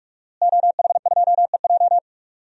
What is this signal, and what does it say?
Morse code, heard as a keyed tone, about 45 wpm, 700 Hz. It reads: OH2EJ
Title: DXCC-100 Call Signs - 5 Characters - Repeated 3x - 45wpm